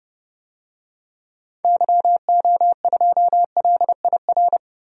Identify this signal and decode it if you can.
Morse code, heard as a keyed tone, about 30 wpm, 700 Hz. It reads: YO2LIR
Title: DXCC-100 Call Signs - 6 Characters - 30wpm